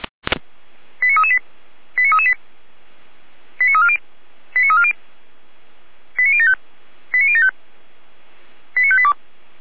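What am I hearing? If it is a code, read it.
Five-tone selective call (ZVEI), heard as quick runs of five tones, tones of about 70 ms. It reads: 89338 89338 89340 89340 89075 89075 89582
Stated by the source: ZVEI five-tone call received by SDR.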